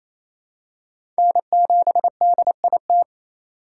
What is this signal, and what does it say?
Morse code, heard as a keyed tone, about 28 wpm, 700 Hz. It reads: N7DIT
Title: USA Call Signs - 5 Characters - 4x Character Spacing - 28wpm